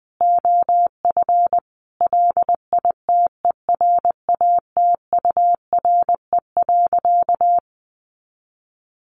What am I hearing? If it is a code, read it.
Morse code, heard as a keyed tone, about 20 wpm, 700 Hz. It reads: OF LITERATURE.
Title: Sentences from Top 900 Words 20wpm